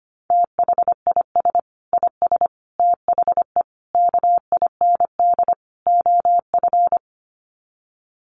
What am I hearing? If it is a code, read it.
Morse code, heard as a keyed tone, about 25 wpm, 700 Hz. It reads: THIS IS THE KIND OF
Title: Sets of 5 Words Encore 25wpm